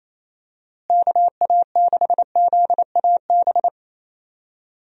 Morse code, heard as a keyed tone, about 28 wpm, 700 Hz. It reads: KA6ZAB
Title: USA Call Signs - 6 Characters - Repeated 2x - 28wpm